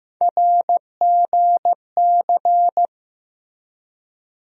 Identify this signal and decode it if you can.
Morse code, heard as a keyed tone, about 16 wpm, 700 Hz. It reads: RGC
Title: USA Call Sign Postfixes - 3 Letters 15wpm